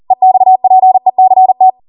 Morse code, heard as a keyed tone, about 40 wpm, 750 Hz. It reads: EXPECT